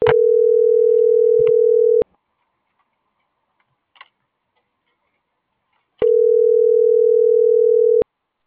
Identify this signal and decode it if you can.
Ringback tone (North America), 440 + 480 Hz.